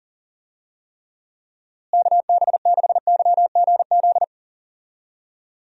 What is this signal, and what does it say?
Morse code, heard as a keyed tone, about 40 wpm, 700 Hz. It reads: KB6YGZ
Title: USA Call Signs - 6 Characters - Repeated 2x - 40wpm